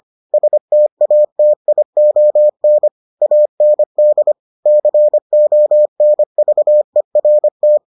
Morse code, heard as a keyed tone, about 25 wpm, 600 Hz. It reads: STATION AND CONVERT